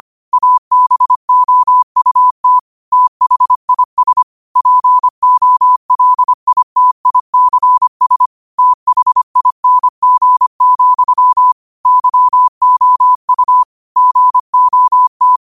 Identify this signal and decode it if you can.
Morse code, heard as a keyed tone, about 25 wpm, 1000 Hz. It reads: ABOUT THIS POLITICS THING, YOU GOT